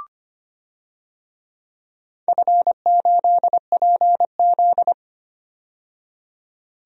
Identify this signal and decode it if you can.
Morse code, heard as a keyed tone, about 25 wpm, 700 Hz. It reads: F8PZ
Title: DXCC-100 Call Signs - 4 Characters - 25wpm